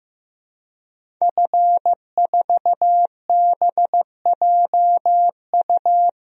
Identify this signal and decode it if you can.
Morse code, heard as a keyed tone, about 15 wpm, 700 Hz. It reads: F4BJU
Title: DXCC-100 Call Signs - 5 Characters - 15wpm